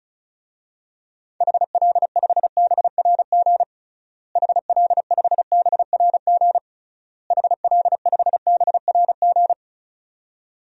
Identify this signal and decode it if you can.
Morse code, heard as a keyed tone, about 36 wpm, 700 Hz. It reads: HL5BRG HL5BRG HL5BRG
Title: DXCC-100 Call Signs - 6 Characters - Repeated 3x - 35wpm